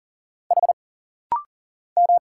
Morse code, heard as a keyed tone, about 40 wpm, 700 Hz. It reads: H M
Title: Single Letter-Number Alt - Rapid-Fire 40wpm